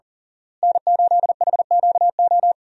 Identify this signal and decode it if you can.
Morse code, heard as a keyed tone, about 38 wpm, 700 Hz. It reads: N8HQO
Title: USA Call Signs - 5 Characters - Repeated 3x - 40wpm